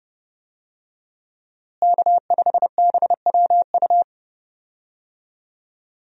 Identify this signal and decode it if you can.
Morse code, heard as a keyed tone, about 30 wpm, 700 Hz. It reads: K5BWU
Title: USA Call Signs - 5 Characters - Repeated 2x - 30wpm